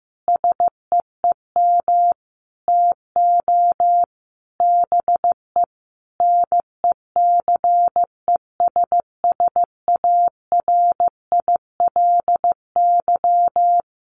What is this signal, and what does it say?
Morse code, heard as a keyed tone, about 15 wpm, 700 Hz. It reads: SEEM TO BE NECESSARILY
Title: Sentences from Top 1400 Words 15wpm